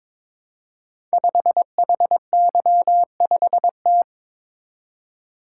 Morse code, heard as a keyed tone, about 22 wpm, 700 Hz. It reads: 5HY5T
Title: Code Groups - 5 Letter:Numbers - 22wpm